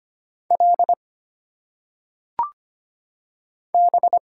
Morse code, heard as a keyed tone, about 25 wpm, 700 Hz. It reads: L B
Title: Single Letters - Phonetic Pronunciation 25wpm